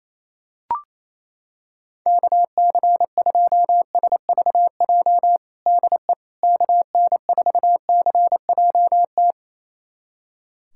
Morse code, heard as a keyed tone, about 28 wpm, 700 Hz. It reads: KC2SVJ DE KN4CJT
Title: USA Callsign DE Callsign - 6 Characters - 28wpm